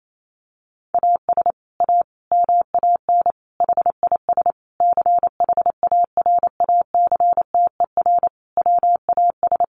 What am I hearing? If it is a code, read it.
Morse code, heard as a keyed tone, about 28 wpm, 700 Hz. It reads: AS A MAN HIS CHARACTER WAS
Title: Sentences from Top 1200 Words 28wpm